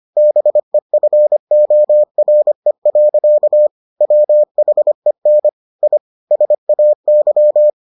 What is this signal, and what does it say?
Morse code, heard as a keyed tone, about 25 wpm, 600 Hz. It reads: BEFORE. WHEN I SAY